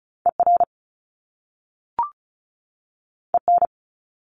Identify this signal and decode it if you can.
Morse code, heard as a keyed tone, about 36 wpm, 700 Hz. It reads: ER EN
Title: Top Two Letter Combinations in Words 35wpm